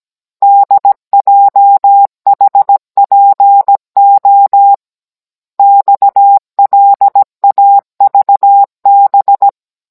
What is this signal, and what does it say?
Morse code, heard as a keyed tone, about 17 wpm, 800 Hz. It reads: DJHPO XLAVB
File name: Cours-NR19-16mm